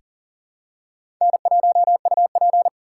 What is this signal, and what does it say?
Morse code, heard as a keyed tone, about 39 wpm, 700 Hz. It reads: N1UP
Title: USA Call Signs - 3x Character Spacing - 40wpm